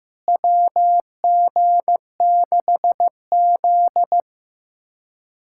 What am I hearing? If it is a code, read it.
Morse code, heard as a keyed tone, about 15 wpm, 700 Hz. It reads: WG6Z
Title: USA Callsign DE Callsign - 4 Characters - 15wpm